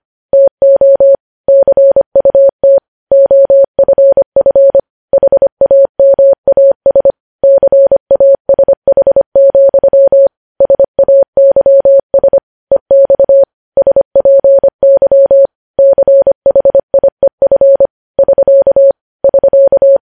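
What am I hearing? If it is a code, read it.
Morse code, heard as a keyed tone, about 25 wpm, 550 Hz. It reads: TO CUT OFF HAMAS CASH, SAYS EX SPY CHIEF <SK> <SK>
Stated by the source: Morse news headlines at 25 WPM.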